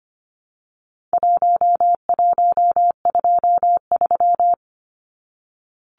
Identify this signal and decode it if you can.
Morse code, heard as a keyed tone, about 25 wpm, 700 Hz. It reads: 1123
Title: Numbers - 4-Digit 25wpm